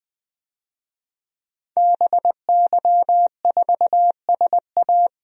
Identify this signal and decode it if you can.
Morse code, heard as a keyed tone, about 20 wpm, 700 Hz. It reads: BY4SA
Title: DXCC-100 Call Signs - 5 Characters - 20wpm